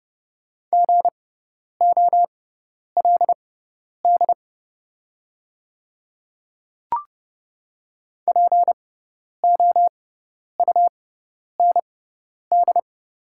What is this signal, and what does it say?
Morse code, heard as a keyed tone, about 29 wpm, spaced as for 10 wpm, 700 Hz. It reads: GOLD POUND
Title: 6x Character Spacing - Top 500 Words - 30wpm